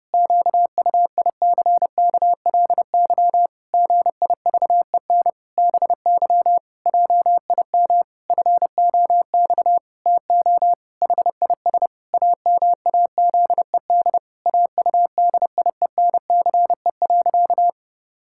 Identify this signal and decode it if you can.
Morse code, heard as a keyed tone, about 30 wpm, 700 Hz. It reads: QUICKLY GIVEN BY JIM FOX TO HIS AMAZED AUDIENCE.
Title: Pangram Challenge v2 - 30wpm